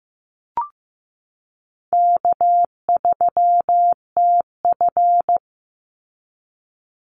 Morse code, heard as a keyed tone, about 14 wpm, 700 Hz. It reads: K3TF
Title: Easy Call Signs Repeated 2x - 15wpm